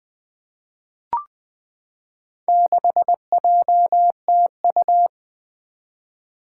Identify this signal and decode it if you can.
Morse code, heard as a keyed tone, about 20 wpm, 700 Hz. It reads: 6JTU
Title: Four Challenging Letter-Number Combinations 20wpm